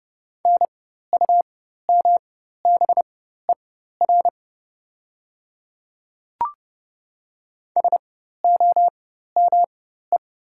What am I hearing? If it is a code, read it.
Morse code, heard as a keyed tone, about 30 wpm, spaced as for 14 wpm, 700 Hz. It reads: NUMBER SOME
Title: 4x Character Spacing - Top 100 Words - 30wpm